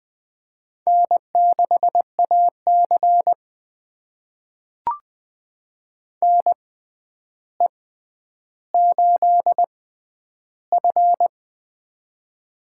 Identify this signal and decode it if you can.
Morse code, heard as a keyed tone, about 20 wpm, spaced as for 10 wpm, 700 Hz. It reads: N6AC NE8F
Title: USA Call Signs - 4 Characters - 6x Character Spacing - 20wpm